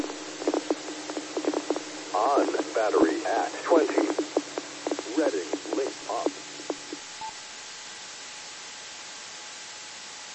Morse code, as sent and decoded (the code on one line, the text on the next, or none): none